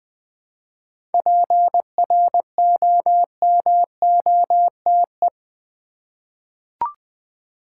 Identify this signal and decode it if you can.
Morse code, heard as a keyed tone, about 20 wpm, 700 Hz. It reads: PROMOTE
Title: 7 Letter Words - First 3 Letters Given - 20wpm